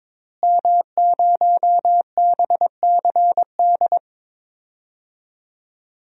Morse code, heard as a keyed tone, about 21 wpm, 700 Hz. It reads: M0BCD
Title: DXCC-100 Call Signs - 5 Characters - Repeated 3x - 22wpm